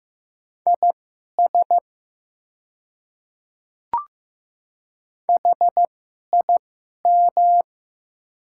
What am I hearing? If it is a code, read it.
Morse code, heard as a keyed tone, about 15 wpm, spaced as for 11 wpm, 700 Hz. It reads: IS HIM
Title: 2x Character Spacing - Top 200 Words - 15wpm